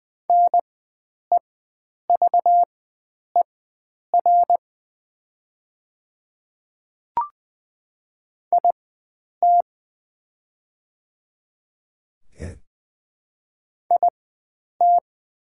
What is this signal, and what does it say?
Morse code, heard as a keyed tone, about 20 wpm, spaced as for 9 wpm, 700 Hz. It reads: NEVER IT IT
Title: 4x Character Spacing - Top 300 Words - 20wpm